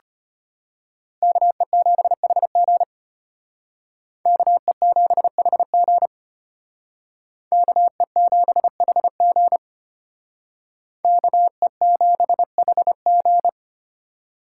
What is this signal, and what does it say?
Morse code, heard as a keyed tone, about 33 wpm, 700 Hz. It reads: KE7HG KE7HG KE7HG KE7HG